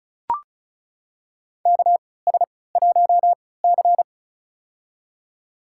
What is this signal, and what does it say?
Morse code, heard as a keyed tone, about 36 wpm, spaced as for 20 wpm, 700 Hz. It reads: KS1C
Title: USA Call Signs - 3x Character Spacing - 35wpm